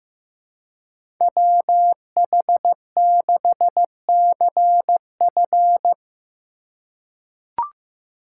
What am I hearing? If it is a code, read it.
Morse code, heard as a keyed tone, about 15 wpm, 700 Hz. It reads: WH6CF
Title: USA Call Signs - 5 Characters - 2x Character Spacing - 15wpm